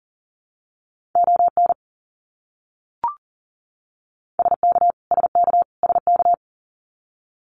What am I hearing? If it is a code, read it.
Morse code, heard as a keyed tone, about 40 wpm, 700 Hz. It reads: ON SK SK SK
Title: Canadian Provinces and Territories - Repeated 3x - 40wpm